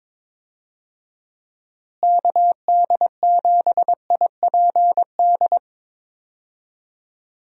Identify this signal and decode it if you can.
Morse code, heard as a keyed tone, about 22 wpm, 700 Hz. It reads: KD7IPD
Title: USA Call Signs - 6 Characters - Repeated 2x - 22wpm